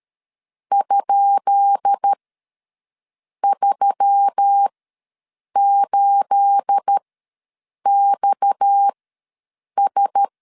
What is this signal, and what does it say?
Morse code, heard as a keyed tone, about 13 wpm, spaced as for 7 wpm, 800 Hz. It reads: ?38XS
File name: LEZIONI / 10 SIGNS / 10B INT ws 12wpm 3527